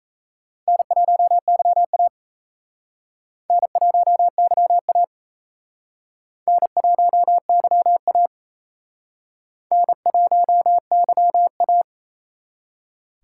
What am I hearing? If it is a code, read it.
Morse code, heard as a keyed tone, about 37 wpm, 700 Hz. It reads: N1YA N1YA N1YA N1YA